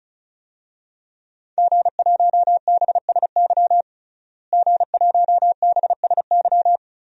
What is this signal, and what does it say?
Morse code, heard as a keyed tone, about 36 wpm, 700 Hz. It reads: G1BSY G1BSY
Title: DXCC-100 Call Signs - 5 Characters - Repeated 3x - 35wpm